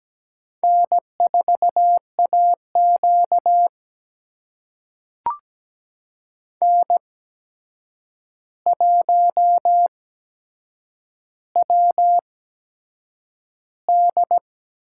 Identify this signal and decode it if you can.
Morse code, heard as a keyed tone, about 17 wpm, spaced as for 7 wpm, 700 Hz. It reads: N4AQ N1WD
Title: USA Call Signs - 4 Characters - 8x Character Spacing - 17wpm